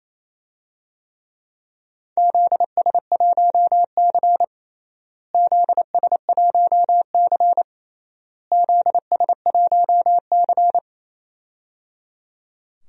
Morse code, heard as a keyed tone, about 27 wpm, 700 Hz. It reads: ZS1C ZS1C ZS1C
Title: DXCC-100 Call Signs - 4 Characters - Repeated 3x - 28wpm